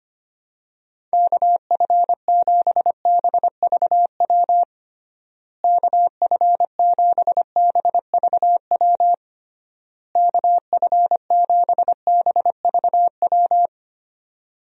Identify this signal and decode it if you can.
Morse code, heard as a keyed tone, about 25 wpm, 700 Hz. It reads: KF7BVW KF7BVW KF7BVW
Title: USA Call Signs - 6 Characters - Repeated 3x - 25wpm